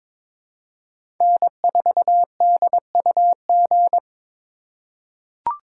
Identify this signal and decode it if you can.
Morse code, heard as a keyed tone, about 22 wpm, 700 Hz. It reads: N4DUG
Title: USA Call Signs - 4x Character Spacing - 22wpm